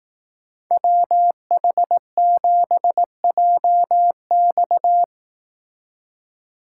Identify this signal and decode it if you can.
Morse code, heard as a keyed tone, about 18 wpm, 700 Hz. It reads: WH7JX
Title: Speed Racing - USA 2x2 Call Signs 15wpm